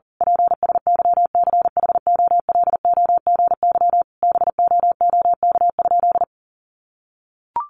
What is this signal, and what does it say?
Morse code, heard as a keyed tone, about 37 wpm, 700 Hz. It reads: PSYCHOLOGY BOOK?
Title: Sentences from Top 1400 Words 40wpm